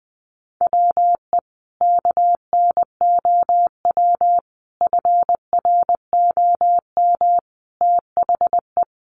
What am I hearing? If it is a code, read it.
Morse code, heard as a keyed tone, about 20 wpm, 700 Hz. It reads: WE KNOW FROM THE
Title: Sentences from Top 500 Words 20wpm